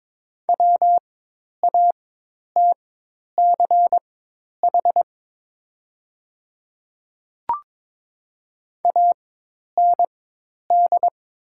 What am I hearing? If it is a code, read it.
Morse code, heard as a keyed tone, about 21 wpm, spaced as for 10 wpm, 700 Hz. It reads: WATCH AND
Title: 4x Character Spacing - Top 300 Words - 22wpm